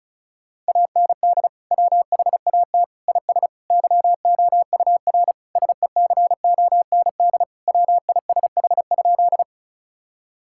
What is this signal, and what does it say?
Morse code, heard as a keyed tone, about 36 wpm, 700 Hz. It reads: AND WHAT IS YOUR SECOND WISH?
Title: Sentences from Top 1600 Words 35wpm